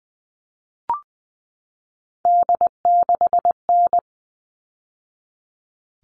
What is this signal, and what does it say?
Morse code, heard as a keyed tone, about 20 wpm, 700 Hz. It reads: D6N
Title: All Cut Numbers - 3-Digit 20wpm